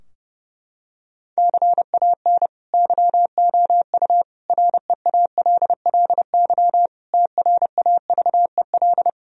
Morse code, heard as a keyed tone, about 32 wpm, 700 Hz. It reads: CAN YOU REALLY TRAVEL
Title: Sentences from Top 1000 Words - Review Part 2 - 30wpm